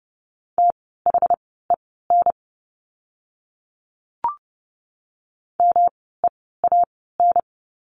Morse code, heard as a keyed tone, about 30 wpm, spaced as for 17 wpm, 700 Hz. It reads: THEN MEAN
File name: course / 30:10wpm / 108 Lesson 18f - Letter m - Words 30wpm